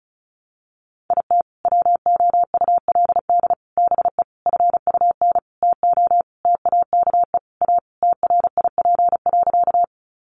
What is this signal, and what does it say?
Morse code, heard as a keyed tone, about 34 wpm, 700 Hz. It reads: IT WOULD BE FUN TO TAKE A TRIP.
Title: Sentences from Top 1300 Words 35wpm